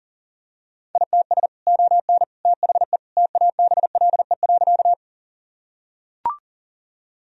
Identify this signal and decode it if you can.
Morse code, heard as a keyed tone, about 37 wpm, 700 Hz. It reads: ITS ON THE TABLE.
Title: Sentences from Top 1000 Words - Review Part 2 - 40wpm